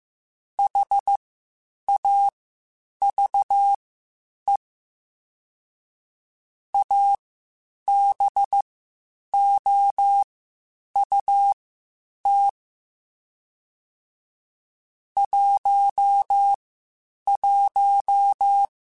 Morse code, heard as a keyed tone, about 15 wpm, spaced as for 8 wpm, 800 Hz. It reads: HAVE ABOUT 11
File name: (spaced) / QSOs Spaced-mp3s / WA7QOX 1 @13 sp